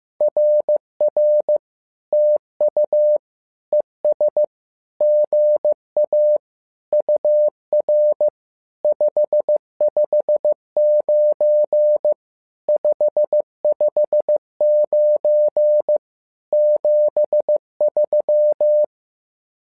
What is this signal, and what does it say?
Morse code, heard as a keyed tone, about 15 wpm, 600 Hz. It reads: RR TU ES GA UR 559 559 73
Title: SOTA QSO Practice 15wpm